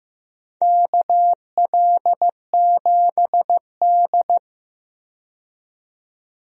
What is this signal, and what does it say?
Morse code, heard as a keyed tone, about 15 wpm, 700 Hz. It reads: KL7D
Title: DXCC-100 Call Signs - 4 Characters - 15wpm